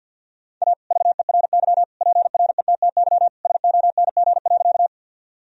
Morse code, heard as a keyed tone, about 52 wpm, 700 Hz. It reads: A VERY PRETTY SONG.
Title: Sentences from Top 1000 Words - Review Part 2 - 50wpm